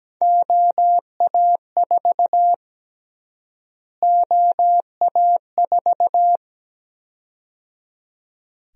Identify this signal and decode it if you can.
Morse code, heard as a keyed tone, about 17 wpm, 700 Hz. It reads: OA4 OA4
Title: USA Call Sign Prefixes - Repeated 2x - 17wpm